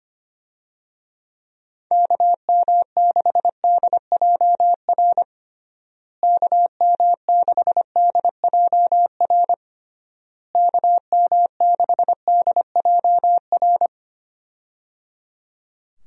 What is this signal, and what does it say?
Morse code, heard as a keyed tone, about 25 wpm, 700 Hz. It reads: KM6DJR KM6DJR KM6DJR
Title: USA Call Signs - 6 Characters - Repeated 3x - 25wpm